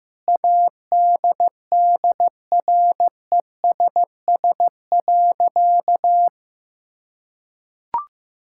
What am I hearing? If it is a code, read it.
Morse code, heard as a keyed tone, about 15 wpm, 700 Hz. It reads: ADDRESS.
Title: Sentences from Top 1000 Words - Review Part 6 - 15wpm